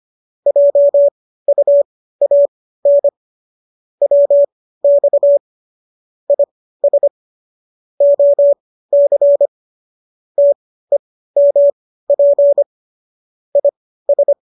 Morse code, heard as a keyed tone, about 25 wpm, spaced as for 15 wpm, 570 Hz. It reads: JUAN WX IS OC TEMP IS
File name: qso204 15